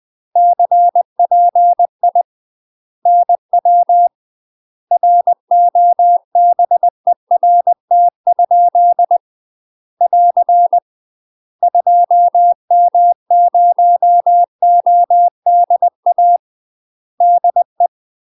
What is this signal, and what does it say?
Morse code, heard as a keyed tone, about 20 wpm, 700 Hz. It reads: CPI NW ROBERT? <AR> 2M0ODA DE
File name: UK MASTER QSOs TRAINING / 20WPM / UK MASTER QSOs 0064